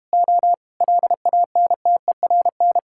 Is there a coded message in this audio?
Yes: Morse code from 0.1 to 2.8 s, heard as a keyed tone, about 31 wpm, 700 Hz. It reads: O LANTERN